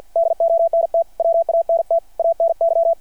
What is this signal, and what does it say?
Morse code, heard as a keyed tone, about 53 wpm, 600 Hz. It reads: DONT WANT ANY
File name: archived / CRSnet